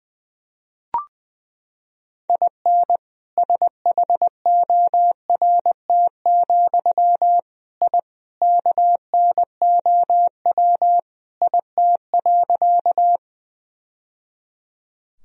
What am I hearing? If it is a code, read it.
Morse code, heard as a keyed tone, about 20 wpm, 700 Hz. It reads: IN SHORT, I KNOW IT.
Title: Sentences from Top 1000 Words - Review Part 2 - 20wpm